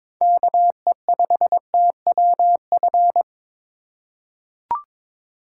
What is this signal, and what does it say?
Morse code, heard as a keyed tone, about 22 wpm, 700 Hz. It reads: KE5TWF
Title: USA Call Signs - 6 Characters - 3x Character Spacing - 22wpm